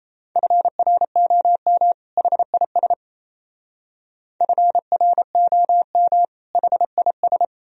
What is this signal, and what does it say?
Morse code, heard as a keyed tone, about 34 wpm, 700 Hz. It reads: FROM HIS FROM HIS
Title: Speed-Racing - Sets of 2 Words 28wpm